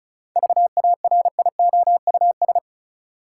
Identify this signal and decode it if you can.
Morse code, heard as a keyed tone, about 36 wpm, 700 Hz. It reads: VARIOUS
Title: Sentences from Top 1000 Words - Review Part 4 - New Word Drill - 35wpm